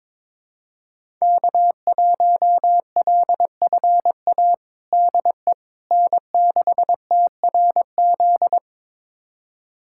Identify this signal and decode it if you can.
Morse code, heard as a keyed tone, about 22 wpm, 700 Hz. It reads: K1LFA DE N6TRZ
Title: USA Callsign DE Callsign - 5 Characters - 22wpm